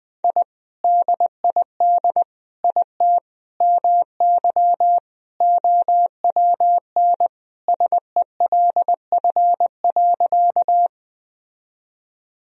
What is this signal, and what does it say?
Morse code, heard as a keyed tone, about 21 wpm, 700 Hz. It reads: I DID IT MY OWN SELF.
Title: Sentences from Top 1000 Words - Review Part 1 - 20wpm